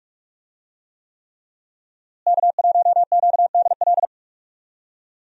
Morse code, heard as a keyed tone, about 42 wpm, 700 Hz. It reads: K1QDL
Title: USA Call Signs - 5 Characters - Repeated 3x - 45wpm